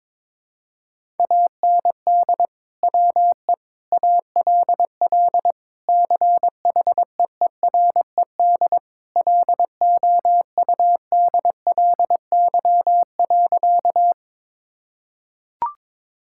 Morse code, heard as a keyed tone, about 21 wpm, 700 Hz. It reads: AND WE ALL CHEERED LOUDLY.
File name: practice / 22wpm / Sentences with 2-Word Phrases - 22wpm